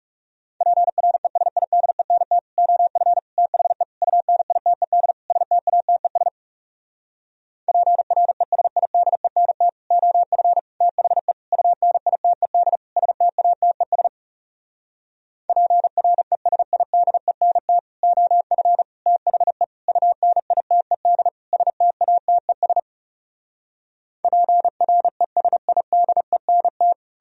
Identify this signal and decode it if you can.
Morse code, heard as a keyed tone, about 40 wpm, 700 Hz. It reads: PRESIDENT OF THE UNITED STATES PRESIDENT OF THE UNITED STATES PRESIDENT OF THE UNITED STATES PRESIDENT